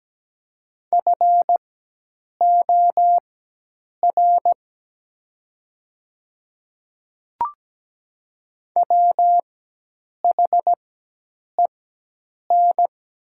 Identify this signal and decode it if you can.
Morse code, heard as a keyed tone, about 17 wpm, spaced as for 8 wpm, 700 Hz. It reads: FOR WHEN
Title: 4x Character Spacing - Top 300 Words - 17wpm